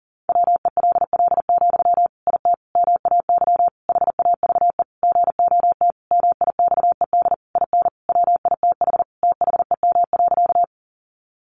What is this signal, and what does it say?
Morse code, heard as a keyed tone, about 40 wpm, 700 Hz. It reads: WELL, IT MAY HAVE GOT MIXED IN WITH THEM.
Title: Sentences from Top 1500 Words 40wpm